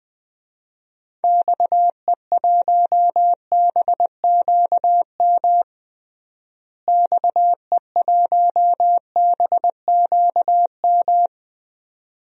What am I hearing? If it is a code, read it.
Morse code, heard as a keyed tone, about 20 wpm, 700 Hz. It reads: XE1BQM XE1BQM
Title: DXCC-100 Call Signs - 6 Characters - Repeated 2x - 20wpm